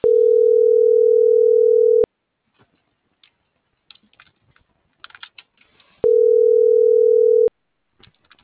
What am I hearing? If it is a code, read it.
Ringback tone (North America), 440 + 480 Hz.
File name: CENTRAL MABE / out miriam.soledad - .75431